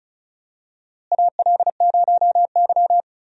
Morse code, heard as a keyed tone, about 36 wpm, 700 Hz. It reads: AL0Y
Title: USA Call Signs - 4 Characters - Repeated 3x - 35wpm